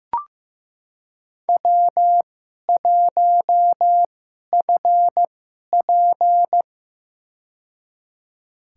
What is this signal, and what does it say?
Morse code, heard as a keyed tone, about 16 wpm, spaced as for 11 wpm, 700 Hz. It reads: W1FP
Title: USA Call Signs - 2x Character Spacing - 15wpm